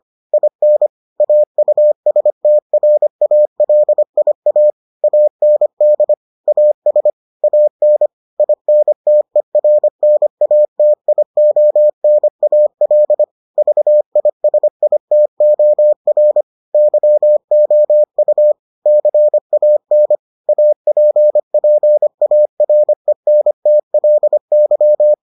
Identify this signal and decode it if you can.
Morse code, heard as a keyed tone, about 25 wpm, 600 Hz. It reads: IN AUSTRALIA AND AS AN INTERNATIONAL VISITOR YOU CAN APPARENTLY